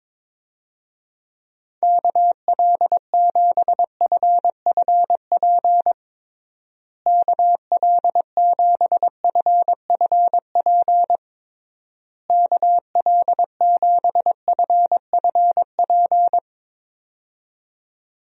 Morse code, heard as a keyed tone, about 22 wpm, 700 Hz. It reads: KL7FFP KL7FFP KL7FFP
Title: USA Call Signs - 6 Characters - Repeated 3x - 22wpm